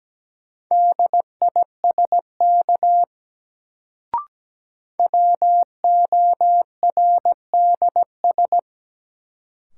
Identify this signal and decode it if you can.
Morse code, heard as a keyed tone, about 17 wpm, 700 Hz. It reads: DISK WORDS
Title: Sentences from Top 1000 Words - Review Part 6 - New Word Drill - 17wpm